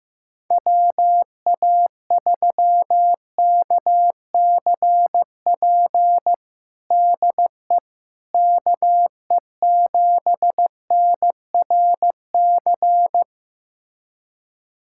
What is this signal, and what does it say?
Morse code, heard as a keyed tone, about 15 wpm, 700 Hz. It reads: WA3KCP DE KE7NRC
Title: USA Callsign DE Callsign - 6 Characters - 15wpm